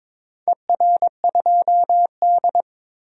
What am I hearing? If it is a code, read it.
Morse code, heard as a keyed tone, about 23 wpm, 700 Hz. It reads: ER2D
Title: DXCC-100 Call Signs - 4 Characters - 22wpm